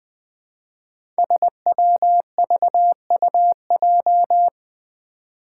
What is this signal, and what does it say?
Morse code, heard as a keyed tone, about 20 wpm, 700 Hz. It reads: SWVUJ
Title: Five Challenging Letter-Number Combinations 20wpm